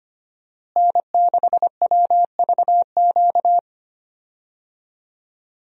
Morse code, heard as a keyed tone, about 25 wpm, 700 Hz. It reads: N6WVQ
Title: USA Call Signs - 5 Characters - Repeated 3x - 25wpm